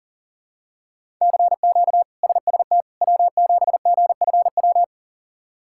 Morse code, heard as a keyed tone, about 40 wpm, 700 Hz. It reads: CQ SST W7GFW
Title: K1USN SST QSO Practice - 40wpm